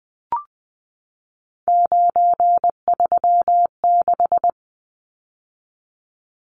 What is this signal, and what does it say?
Morse code, heard as a keyed tone, about 20 wpm, 700 Hz. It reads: 936
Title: Numbers - 3-Digit 20wpm